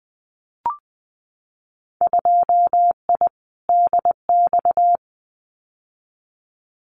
Morse code, heard as a keyed tone, about 20 wpm, 700 Hz. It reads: 2I DX
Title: Winter Field Day Exchanges 20wpm